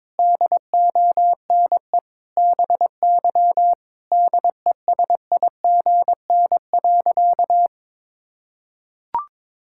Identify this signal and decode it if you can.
Morse code, heard as a keyed tone, about 22 wpm, 700 Hz. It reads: DONE BY DESIGN.
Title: Sentences from Top 1000 Words - Review Part 5 - 22wpm